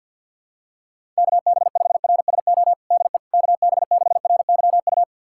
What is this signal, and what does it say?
Morse code, heard as a keyed tone, about 46 wpm, 700 Hz. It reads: KB5RSO DE KB6RYU
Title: USA Callsign DE Callsign - 6 Characters - 50wpm